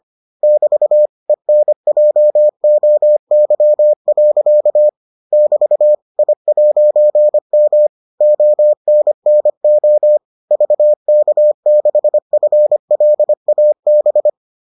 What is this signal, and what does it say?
Morse code, heard as a keyed tone, about 25 wpm, 600 Hz. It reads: <BT> ENJOY. <BT> I'M ONNO VK6FLAB